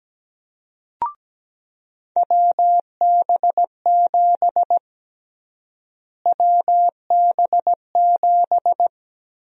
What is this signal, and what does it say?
Morse code, heard as a keyed tone, about 17 wpm, 700 Hz. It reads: WB7 WB7
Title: USA Call Sign Prefixes - Repeated 2x - 17wpm